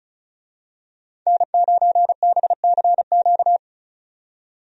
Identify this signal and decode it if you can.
Morse code, heard as a keyed tone, about 34 wpm, 700 Hz. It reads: N9BCQ
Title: USA Call Signs - 5 Characters - 35wpm